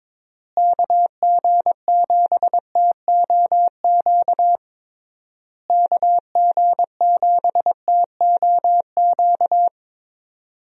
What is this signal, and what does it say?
Morse code, heard as a keyed tone, about 22 wpm, 700 Hz. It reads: KG7TOQ KG7TOQ
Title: USA Call Signs - 6 Characters - Repeated 3x - 22wpm